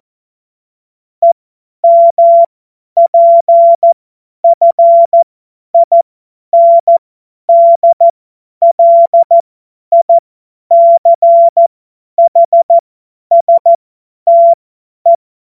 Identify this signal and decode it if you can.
Morse code, heard as a keyed tone, about 14 wpm, spaced as for 10 wpm, 680 Hz. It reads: EMPFINDLICHSTE
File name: Morsekurs / Witze / Witz 05 Dezember 2023 50 F70